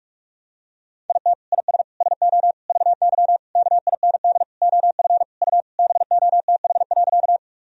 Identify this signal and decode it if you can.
Morse code, heard as a keyed tone, about 47 wpm, 700 Hz. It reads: IT IS SO VY KIND OF U BOTH.
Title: Sentences from Top 300 Words and Abbreviations 45wpm